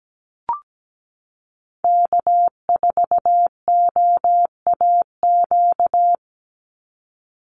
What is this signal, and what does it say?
Morse code, heard as a keyed tone, about 17 wpm, 700 Hz. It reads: K4OAQ
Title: Call Signs Repeated 3x - 17wpm